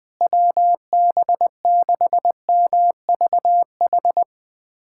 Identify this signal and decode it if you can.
Morse code, heard as a keyed tone, about 20 wpm, 700 Hz. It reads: WB6MVH
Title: USA Callsign DE Callsign - 6 Characters - 20wpm